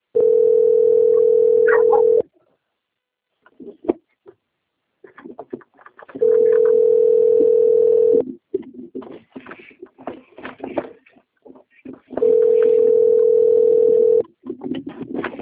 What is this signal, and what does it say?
Ringback tone (North America), 440 + 480 Hz.
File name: out- - - - - .2232